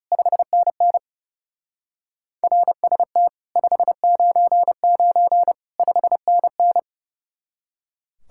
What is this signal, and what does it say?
Morse code, heard as a keyed tone, about 37 wpm, 700 Hz. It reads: 5NN RST 599 5NN
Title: Speed-Racing Common QSO Phrases - 30wpm